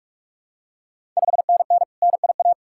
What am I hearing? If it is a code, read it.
Morse code, heard as a keyed tone, about 46 wpm, 700 Hz. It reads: 5NN NIA
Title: Ontario QSO Party Exchange 45wpm